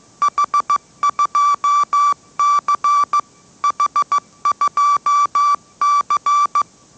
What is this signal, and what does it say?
Morse code, heard as a keyed tone, about 15 wpm, 1160 Hz. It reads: H2C H2C
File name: mrx / mrx065 16wpm